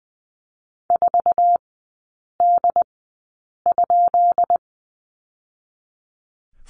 Morse code, heard as a keyed tone, about 20 wpm, spaced as for 8 wpm, 690 Hz. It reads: 4D?
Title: 159 Lesson 26e - Letter g - Three Characters 20wpm